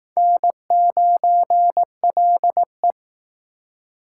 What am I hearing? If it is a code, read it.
Morse code, heard as a keyed tone, about 19 wpm, 700 Hz. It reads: N9LE